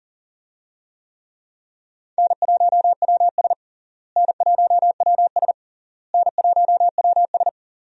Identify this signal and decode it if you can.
Morse code, heard as a keyed tone, about 40 wpm, 700 Hz. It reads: N1WS N1WS N1WS
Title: USA Call Signs - 4 Characters - Repeated 3x - 40wpm